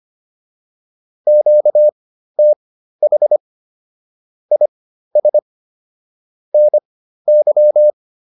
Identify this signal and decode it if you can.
Morse code, heard as a keyed tone, about 25 wpm, spaced as for 13 wpm, 600 Hz. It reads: QTH IS NY
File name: QSO106 13